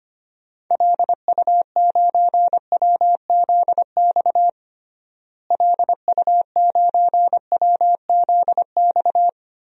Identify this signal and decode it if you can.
Morse code, heard as a keyed tone, about 25 wpm, 700 Hz. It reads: LU9WZX LU9WZX
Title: DXCC-100 Call Signs - 6 Characters - Repeated 3x - 25wpm